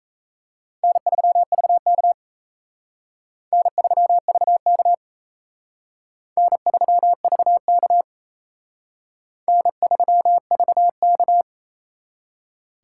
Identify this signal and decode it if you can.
Morse code, heard as a keyed tone, about 35 wpm, 700 Hz. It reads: N3VK N3VK N3VK N3VK
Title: Speed Racing - USA Call Signs 28wpm